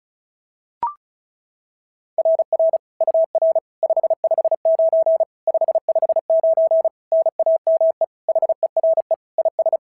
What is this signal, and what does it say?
Morse code, heard as a keyed tone, about 34 wpm, 660 Hz. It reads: RR UR 559 559 NAME HERE IS
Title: SOTA QSO Practice 35wpm